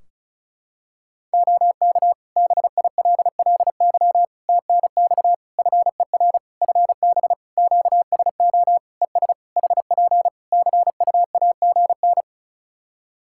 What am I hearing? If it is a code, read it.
Morse code, heard as a keyed tone, about 34 wpm, 700 Hz. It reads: OK BILLY TNX FER FB QSO ES HP CUAGN
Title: Practice QSO - Exchange 4 - Ending - 35wpm